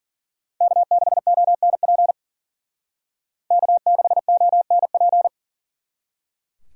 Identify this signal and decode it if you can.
Morse code, heard as a keyed tone, about 48 wpm, 700 Hz. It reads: K6ONP K6ONP
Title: Speed Racing - USA 1x3 Call Signs 40wpm